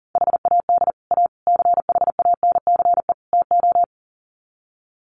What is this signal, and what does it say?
Morse code, heard as a keyed tone, about 40 wpm, 700 Hz. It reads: HAD A CHANCE TO
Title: Sets of 4 Words 40wpm